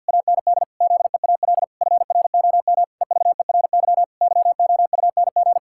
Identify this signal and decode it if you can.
Morse code, heard as a keyed tone, about 50 wpm, 700 Hz. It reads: AND ZEAL FROM EVERY YOUNG
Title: Pangram Challenge v2 - 50wpm